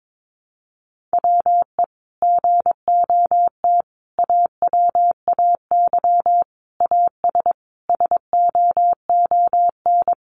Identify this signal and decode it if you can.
Morse code, heard as a keyed tone, about 22 wpm, 700 Hz. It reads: WE GOT AWAY AS SOON
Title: Sentences from Top 600 Words 22wpm